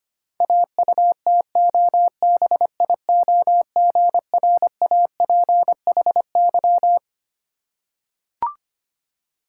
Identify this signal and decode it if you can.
Morse code, heard as a keyed tone, about 25 wpm, 700 Hz. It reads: AUTOBIOGRAPHY
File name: practice / 25wpm / Difficult Words with Hints - 25wpm